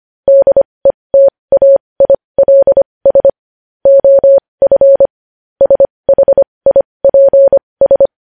Morse code, heard as a keyed tone, about 25 wpm, spaced as for 20 wpm, 550 Hz. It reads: DETAILS OF SHIPS
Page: news headlines in morse code at 20 wpm